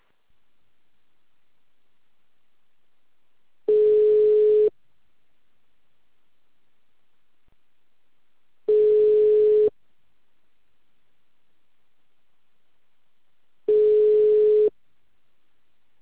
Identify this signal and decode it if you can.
Ringback tone (Europe, 425 Hz).
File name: out- -2044- - - .59417